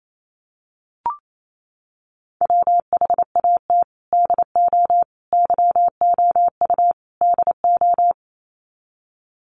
Morse code, heard as a keyed tone, about 28 wpm, 700 Hz. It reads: WHAT DO YOU DO
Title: Sets of 4 Words 28wpm